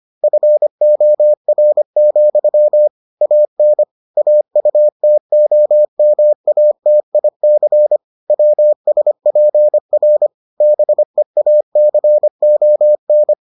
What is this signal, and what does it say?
Morse code, heard as a keyed tone, about 25 wpm, 600 Hz. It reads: FOR, AN AUTOMATIC WSPR BEACON